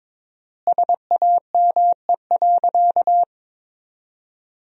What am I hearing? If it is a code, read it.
Morse code, heard as a keyed tone, about 22 wpm, 700 Hz. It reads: SAME.
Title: 4x Word-Spacing - Sentences from Top 300 Words - 22wpm